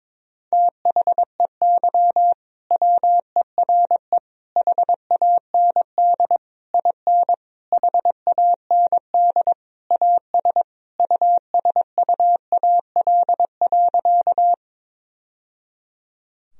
Morse code, heard as a keyed tone, about 22 wpm, 700 Hz. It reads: THEY WERE HAND IN HAND AS USUAL.
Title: Sentences from Top 1000 Words - Review Part 2 - 22wpm